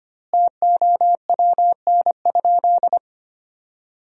Morse code, heard as a keyed tone, about 25 wpm, 700 Hz. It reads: TOWN?
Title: 2x Word-Spacing - Sentences from Top 500 Words - 25wpm